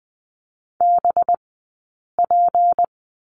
Morse code, heard as a keyed tone, about 20 wpm, spaced as for 8 wpm, 700 Hz. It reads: BP
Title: 211 Lesson 34d - Number 8 - Two Characters 20wpm